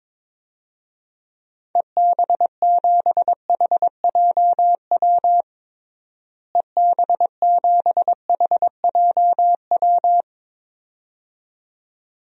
Morse code, heard as a keyed tone, about 22 wpm, 700 Hz. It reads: EB7HJW EB7HJW
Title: DXCC-100 Call Signs - 6 Characters - Repeated 2x - 22wpm